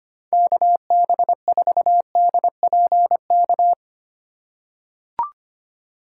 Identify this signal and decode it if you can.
Morse code, heard as a keyed tone, about 25 wpm, 700 Hz. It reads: KB4DPK
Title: USA Call Signs - 4x Character Spacing - 25wpm